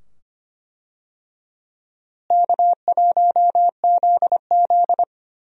Morse code, heard as a keyed tone, about 25 wpm, 700 Hz. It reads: K1ZZ
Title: USA Call Signs - 4 Characters - Repeated 2x - 25wpm